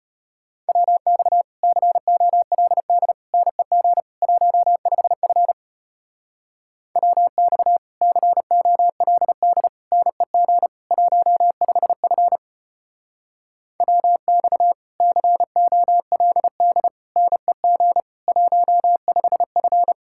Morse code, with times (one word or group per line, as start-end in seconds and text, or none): 0.7-1.4 WX
1.6-3.1 COLD
3.3-4.0 NEG
4.2-5.5 15F
7.0-7.8 WX
8.0-9.7 COLD
9.9-10.7 NEG
10.9-12.4 15F
13.8-14.7 WX
15.0-16.9 COLD
17.2-18.0 NEG
18.3-19.9 15F